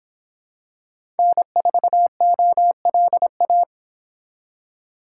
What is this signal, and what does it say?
Morse code, heard as a keyed tone, about 25 wpm, 700 Hz. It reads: N4OLA